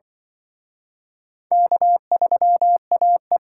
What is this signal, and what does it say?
Morse code, heard as a keyed tone, about 25 wpm, 700 Hz. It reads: K3AE